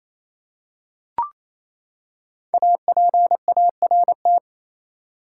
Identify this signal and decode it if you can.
Morse code, heard as a keyed tone, about 28 wpm, 700 Hz. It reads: APART
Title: Five Letter Words - 28wpm